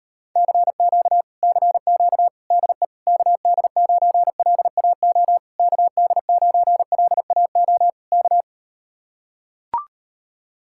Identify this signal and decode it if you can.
Morse code, heard as a keyed tone, about 38 wpm, 700 Hz. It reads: CQ CQ DE KD9LAO KD9LAO K